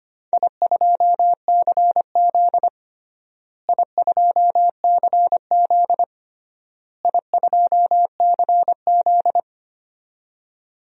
Morse code, heard as a keyed tone, about 25 wpm, 700 Hz. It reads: I2CZ I2CZ I2CZ
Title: DXCC-100 Call Signs - 4 Characters - Repeated 3x - 25wpm